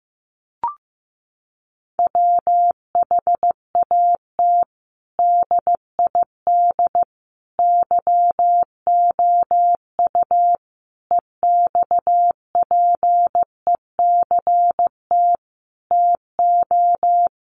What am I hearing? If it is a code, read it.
Morse code, heard as a keyed tone, about 15 wpm, 700 Hz. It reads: WHAT DID YOU EXPECT TO
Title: Sentences from Top 1100 Words 15wpm